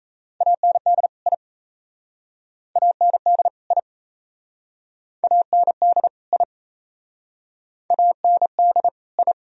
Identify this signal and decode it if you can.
Morse code, heard as a keyed tone, about 38 wpm, 700 Hz. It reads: AND I AND I AND I AND I